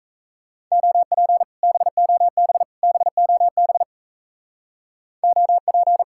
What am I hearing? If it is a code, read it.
Morse code, heard as a keyed tone, about 36 wpm, 700 Hz. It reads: OP BOB BOB OP